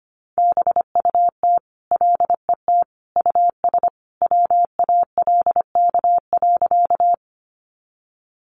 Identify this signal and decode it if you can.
Morse code, heard as a keyed tone, about 25 wpm, 700 Hz. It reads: BUT LET US WALK.
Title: Sentences from Top 300 Words 25wpm